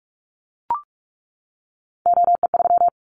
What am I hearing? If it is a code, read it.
Morse code, heard as a keyed tone, about 44 wpm, 700 Hz. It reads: OE3
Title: Callsign Prefixes for Top 100 DXCC - 45wpm